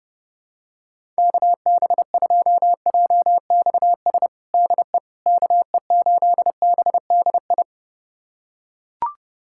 Morse code, heard as a keyed tone, about 30 wpm, 700 Hz. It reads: KB2JXS DE KE8BDI
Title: USA Callsign DE Callsign 30wpm